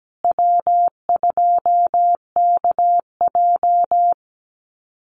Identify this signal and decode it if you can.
Morse code, heard as a keyed tone, about 17 wpm, 700 Hz. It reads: W2KJ
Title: Call Signs Repeated 3x - 17wpm